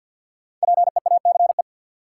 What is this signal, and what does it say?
Morse code, heard as a keyed tone, about 53 wpm, 700 Hz. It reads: PEACE
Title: Sentences from Top 1500 Words - New Word Drill Alt - 50wpm